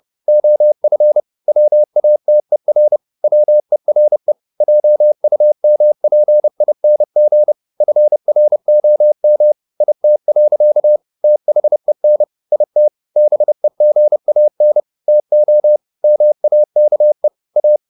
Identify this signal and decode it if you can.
Morse code, heard as a keyed tone, about 30 wpm, 600 Hz. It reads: OF WATER WERE JUMPING FROM IT. THEN IT BEGAN TO MAKE A
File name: the old man and the sea / the old man and the sea 1 30wpm